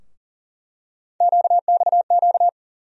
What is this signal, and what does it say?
Morse code, heard as a keyed tone, about 40 wpm, 700 Hz. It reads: QXQ
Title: USA Call Sign Postfixes - 3 Letters 40wpm